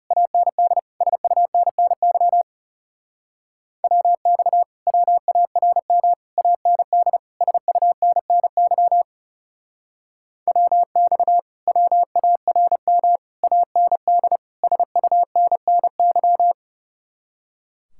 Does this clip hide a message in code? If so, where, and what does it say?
Yes: Morse code from 0.1 to 16.5 s, heard as a keyed tone, about 38 wpm, 700 Hz. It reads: AND SUNNY WX WARM AND SUNNY WX WARM AND SUNNY